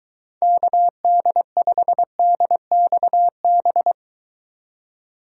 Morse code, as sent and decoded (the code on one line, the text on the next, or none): -.- -.. ..... -.. -..- -...
KD5DXB